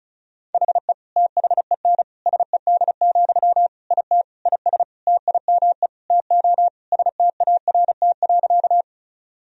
Morse code, heard as a keyed tone, about 36 wpm, 700 Hz. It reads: HE THEN SED, IT IS TIME TO START.
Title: Sentences from Top 300 Words and Abbreviations 35wpm